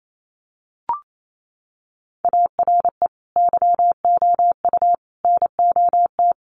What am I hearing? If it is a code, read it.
Morse code, heard as a keyed tone, about 28 wpm, 700 Hz. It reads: ARE YOU NOT
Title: Sentences from Top 1000 Words 28wpm